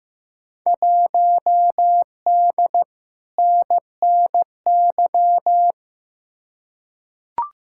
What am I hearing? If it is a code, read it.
Morse code, heard as a keyed tone, about 15 wpm, 700 Hz. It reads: 1D NNY
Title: Field Day Exchanges - 15wpm